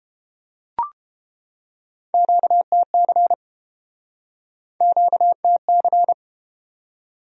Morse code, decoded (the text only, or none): QTC QTC